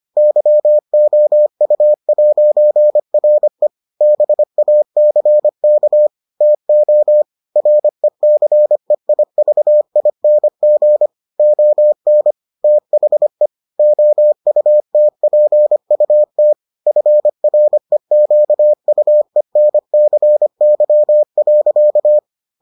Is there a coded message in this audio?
Yes: Morse code from 0.2 to 22.2 s, heard as a keyed tone, about 25 wpm, 600 Hz. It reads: YOU'RE BACK TO RECEIVING ON THE OUTPUT FREQUENCY.